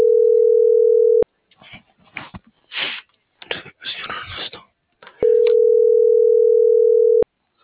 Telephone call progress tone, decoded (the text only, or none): ringback tone